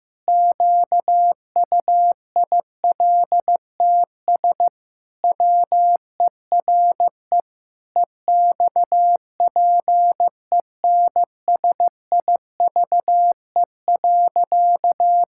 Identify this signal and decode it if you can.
Morse code, heard as a keyed tone, about 15 wpm, 700 Hz. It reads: QUILTS WERE EXPENSIVE.
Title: Pangram Challenge v2 - 15wpm